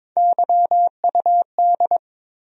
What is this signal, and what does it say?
Morse code, heard as a keyed tone, about 22 wpm, 700 Hz. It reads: YUD
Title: USA Call Sign Postfixes - 3 Letters 22wpm